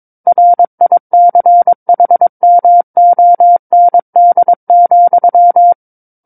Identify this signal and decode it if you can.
Morse code, heard as a keyed tone, about 22 wpm, 700 Hz. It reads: RICHMOND,
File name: Book One / 13-Chapter 13 - 22 WPM